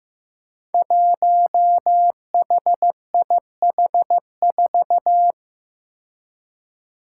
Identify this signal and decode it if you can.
Morse code, heard as a keyed tone, about 15 wpm, 700 Hz. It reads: 1HIH4
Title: Five Challenging Letter-Number Combinations 15wpm